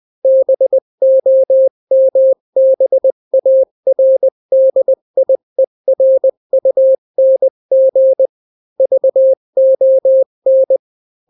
Morse code, heard as a keyed tone, about 20 wpm, 530 Hz. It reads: BOMBARDIERUNG VON
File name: daily / 18wpm